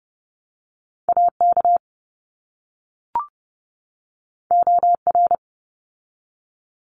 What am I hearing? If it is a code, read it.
Morse code, heard as a keyed tone, about 30 wpm, 700 Hz. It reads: AK OR
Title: ARRL Sections 30wpm